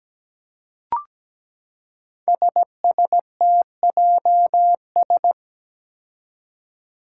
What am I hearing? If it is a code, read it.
Morse code, heard as a keyed tone, about 17 wpm, 700 Hz. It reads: SSTJS
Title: Five Challenging Letter-Number Combinations 17wpm